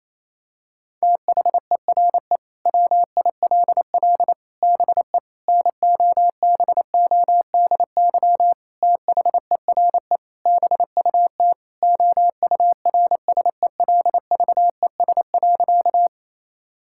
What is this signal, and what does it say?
Morse code, heard as a keyed tone, about 28 wpm, 700 Hz. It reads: THERE WILL BE NOBODY THERE BUT OURSELVES.
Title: Sentences from Top 1600 Words 28wpm